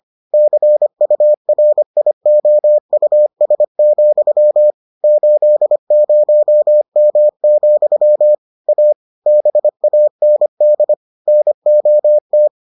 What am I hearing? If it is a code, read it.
Morse code, heard as a keyed tone, about 25 wpm, 600 Hz. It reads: CURIOUS, 80M, A BAND NOT